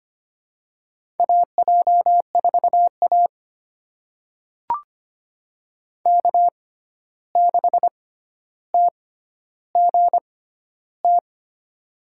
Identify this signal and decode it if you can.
Morse code, heard as a keyed tone, about 25 wpm, spaced as for 12 wpm, 700 Hz. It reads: AJ4A K6TGT
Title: USA Call Signs - 6x Character Spacing - 25wpm